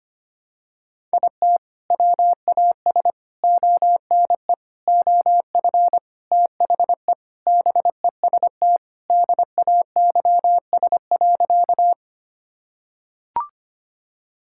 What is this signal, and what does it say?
Morse code, heard as a keyed tone, about 25 wpm, 700 Hz. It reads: IT WAS ONE OF THE BEST DAYS.
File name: practice / 25wpm / Sentences with 3-Word Phrases - 25wpm